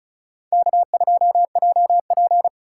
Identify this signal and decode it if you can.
Morse code, heard as a keyed tone, about 46 wpm, 700 Hz. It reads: K2JP